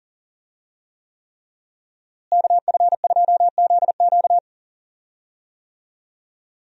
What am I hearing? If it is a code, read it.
Morse code, heard as a keyed tone, about 37 wpm, 700 Hz. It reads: KF2ZQ
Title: USA Call Signs - 5 Characters - Repeated 3x - 40wpm